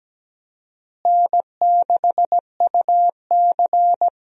Morse code, heard as a keyed tone, about 17 wpm, 700 Hz. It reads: N6UC
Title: USA Call Signs - 4 Characters - Repeated 3x - 17wpm